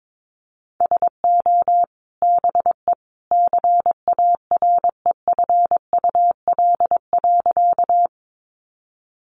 Morse code, heard as a keyed tone, about 22 wpm, 700 Hz. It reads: SO BE CAREFUL.
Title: A Sound Of Thunder 22wpm